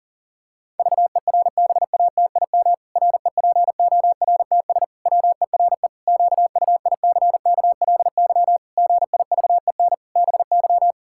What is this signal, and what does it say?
Morse code, heard as a keyed tone, about 40 wpm, 700 Hz. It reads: VERBATIM REPORTS WERE QUICKLY GIVEN BY
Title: Pangram Challenge v2 - 40wpm